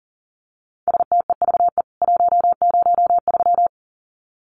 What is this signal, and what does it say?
Morse code, heard as a keyed tone, about 40 wpm, 700 Hz. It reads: STEVE 103
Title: CWT Exchange 40wpm